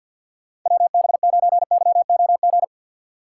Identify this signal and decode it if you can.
Morse code, heard as a keyed tone, about 51 wpm, 700 Hz. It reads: WB9YOG
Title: USA Call Signs - 6 Characters - 4x Character Spacing - 50wpm